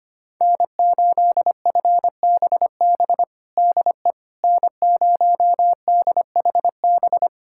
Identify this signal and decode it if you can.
Morse code, heard as a keyed tone, about 25 wpm, 700 Hz. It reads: N8FBB DE N0DHB
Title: USA Callsign DE Callsign - 5 Characters - 25wpm